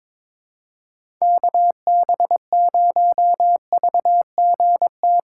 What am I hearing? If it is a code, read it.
Morse code, heard as a keyed tone, about 22 wpm, 700 Hz. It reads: KB0VGT
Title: USA Call Signs - 6 Characters - 4x Character Spacing - 22wpm